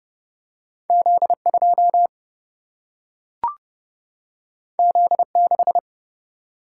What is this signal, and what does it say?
Morse code, heard as a keyed tone, about 30 wpm, 700 Hz. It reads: Z 2 Z 6
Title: Single-Letter Single-Number 30wpm